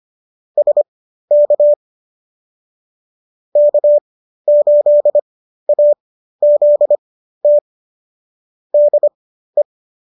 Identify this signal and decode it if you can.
Morse code, heard as a keyed tone, about 25 wpm, spaced as for 13 wpm, 590 Hz. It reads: SK K8AZT DE